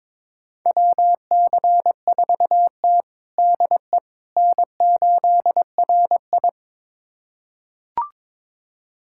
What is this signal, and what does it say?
Morse code, heard as a keyed tone, about 22 wpm, 700 Hz. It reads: WC4T DE N8RI
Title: USA Callsign DE Callsign - 4 Characters - 22wpm